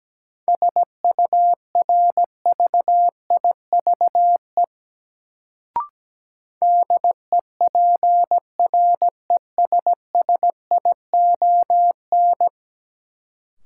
Morse code, heard as a keyed tone, about 17 wpm, 700 Hz. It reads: SURVIVE DEPRESSION
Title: Sentences from Top 1300 Words - New Word Drill - 17wpm